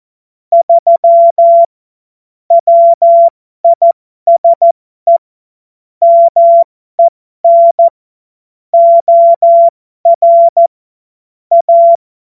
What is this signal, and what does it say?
Morse code, heard as a keyed tone, about 14 wpm, spaced as for 12 wpm, 680 Hz. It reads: 3 WISE MEN OR A
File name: Morsekurs / Witze